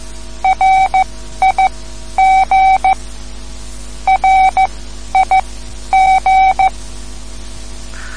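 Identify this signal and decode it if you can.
Morse code, heard as a keyed tone, about 14 wpm, spaced as for 11 wpm, 760 Hz. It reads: RIG RIG
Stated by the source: Morse code lesson, third lesson in a series for learning the Morse alphabet.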